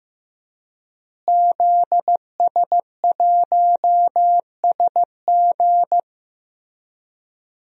Morse code, heard as a keyed tone, about 15 wpm, 700 Hz. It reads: ZS1SG
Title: DXCC-100 Call Signs - 5 Characters - Repeated 2x - 15wpm